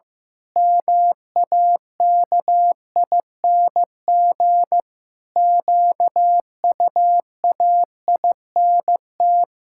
Morse code, heard as a keyed tone, about 15 wpm, 700 Hz. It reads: MAKING QUAINT
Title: Pangram Challenge v2 - 15wpm